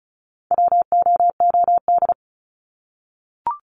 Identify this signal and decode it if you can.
Morse code, heard as a keyed tone, about 34 wpm, 700 Hz. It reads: WOOD
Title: Top 300 Words 35wpm